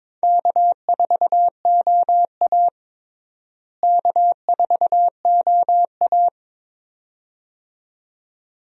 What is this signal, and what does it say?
Morse code, heard as a keyed tone, about 22 wpm, 700 Hz. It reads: K4OA K4OA
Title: USA Call Signs - 4 Characters - Repeated 2x - 22wpm